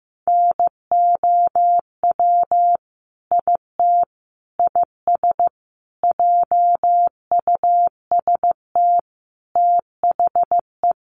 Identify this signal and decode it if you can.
Morse code, heard as a keyed tone, about 15 wpm, 700 Hz. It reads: NOW IT IS JUST THE
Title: Sentences from Top 1300 Words 15wpm